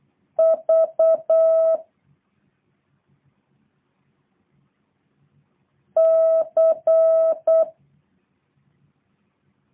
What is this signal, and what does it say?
Morse code, heard as a keyed tone, about 8 wpm, 650 Hz. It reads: V C